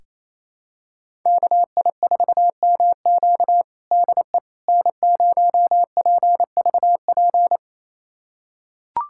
Morse code, heard as a keyed tone, about 28 wpm, 700 Hz. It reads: KI4MQ DE N0PVP
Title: USA Callsign DE Callsign - 5 Characters - 28wpm